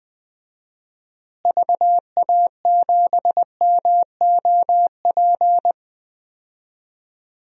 Morse code, heard as a keyed tone, about 20 wpm, 700 Hz. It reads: VA7MOP
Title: DXCC-100 Call Signs - 6 Characters - Repeated 3x - 20wpm